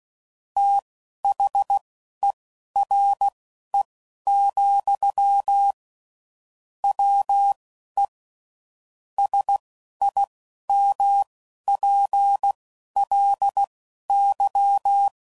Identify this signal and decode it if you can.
Morse code, heard as a keyed tone, about 16 wpm, spaced as for 11 wpm, 800 Hz. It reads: THERE, WE SIMPLY
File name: CW Code Groups @10WPM / Technical Subjects / SwPwrSup